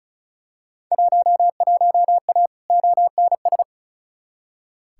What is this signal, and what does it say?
Morse code, heard as a keyed tone, about 36 wpm, 700 Hz. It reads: 11A ONS
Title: Field Day Exchanges Encore - 35wpm